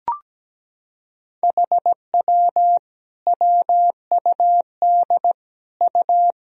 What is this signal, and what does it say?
Morse code, heard as a keyed tone, about 17 wpm, 700 Hz. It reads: HW WUD U
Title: Sentences from Top 100 Words and Abbreviations 17wpm